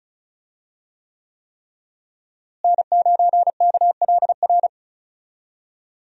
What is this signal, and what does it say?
Morse code, heard as a keyed tone, about 36 wpm, 700 Hz. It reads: N9KLR
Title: USA Call Signs - 5 Characters - Repeated 3x - 35wpm